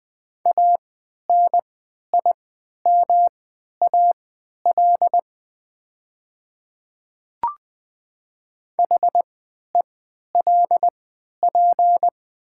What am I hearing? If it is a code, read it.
Morse code, heard as a keyed tone, about 20 wpm, spaced as for 11 wpm, 700 Hz. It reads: ANIMAL HELP
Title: 3x Character Spacing - Top 200 Words - 20wpm